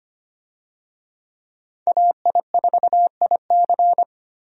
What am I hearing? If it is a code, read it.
Morse code, heard as a keyed tone, about 25 wpm, 700 Hz. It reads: AI4IC
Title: USA Call Signs - 5 Characters - 25wpm